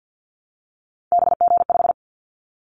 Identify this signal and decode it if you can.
Morse code, heard as a keyed tone, about 48 wpm, 700 Hz. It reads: BG5
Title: Callsign Prefixes for Top 100 DXCC - 50wpm